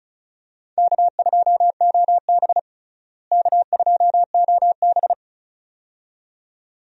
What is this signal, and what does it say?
Morse code, heard as a keyed tone, about 36 wpm, 700 Hz. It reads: K2OB K2OB
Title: USA Call Signs - 4 Characters - Repeated 2x - 35wpm